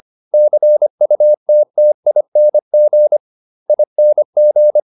Morse code, heard as a keyed tone, about 25 wpm, 600 Hz. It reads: CUTTING ING